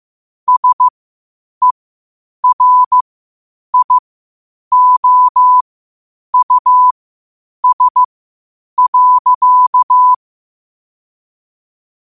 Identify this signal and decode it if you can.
Morse code, heard as a keyed tone, about 15 wpm, spaced as for 8 wpm, 1000 Hz. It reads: SERIOUS.
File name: qotdfiles / 8wpm